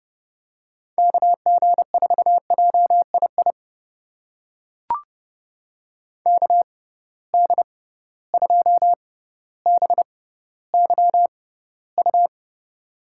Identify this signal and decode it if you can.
Morse code, heard as a keyed tone, about 30 wpm, spaced as for 15 wpm, 700 Hz. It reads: KG4JII KD2BYU
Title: USA Call Signs - 6 Characters - 6x Character Spacing - 30wpm